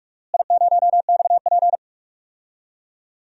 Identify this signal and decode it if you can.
Morse code, heard as a keyed tone, about 45 wpm, 700 Hz. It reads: I0XP
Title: DXCC-100 Call Signs - 4 Characters - Repeated 2x - 45wpm